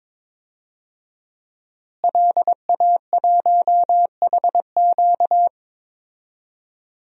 Morse code, heard as a keyed tone, about 22 wpm, 700 Hz. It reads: LA1HQ
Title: DXCC-100 Call Signs - 5 Characters - Repeated 3x - 22wpm